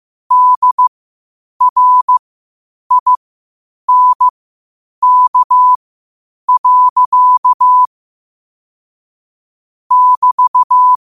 Morse code, heard as a keyed tone, about 15 wpm, spaced as for 8 wpm, 1000 Hz. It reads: DRINK. <BT>
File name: qotdfiles / 8wpm